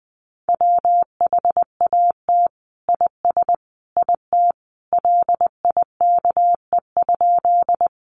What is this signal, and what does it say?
Morse code, heard as a keyed tone, about 20 wpm, 700 Hz. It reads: WHAT IS IT LIKE?
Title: Sentences from Top 100 Words 20wpm